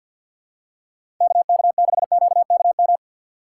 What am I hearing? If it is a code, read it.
Morse code, heard as a keyed tone, about 49 wpm, 700 Hz. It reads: KK6QKM
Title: USA Call Signs - 6 Characters - 4x Character Spacing - 50wpm